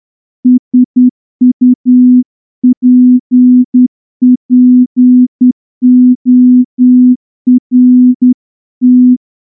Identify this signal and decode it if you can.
Morse code, heard as a keyed tone, about 10 wpm, 250 Hz. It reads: SUPPORT
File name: morse-training-10WPM